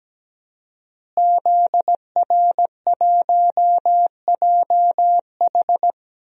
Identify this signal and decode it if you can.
Morse code, heard as a keyed tone, about 17 wpm, 700 Hz. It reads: ZR1JH
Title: DXCC-100 Call Signs - 5 Characters - Repeated 2x - 17wpm